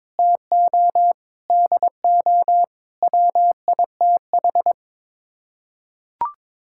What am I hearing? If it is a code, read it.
Morse code, heard as a keyed tone, about 22 wpm, 700 Hz. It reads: TO DO WITH
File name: practice / 22wpm / Build A Phrase - 5-Word Phrases - 22wpm